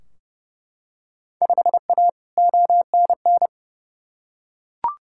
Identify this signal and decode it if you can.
Morse code, heard as a keyed tone, about 30 wpm, 700 Hz. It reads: 5A ONN
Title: Field Day Exchanges Encore - 30wpm